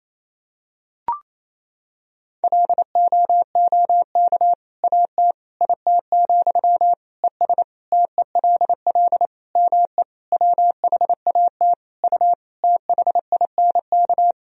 Morse code, heard as a keyed tone, about 28 wpm, 700 Hz. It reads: LOOK AT IT, ES TELL ME WHAT U THINK
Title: Sentences from Top 200 Words and Abbreviations 28wpm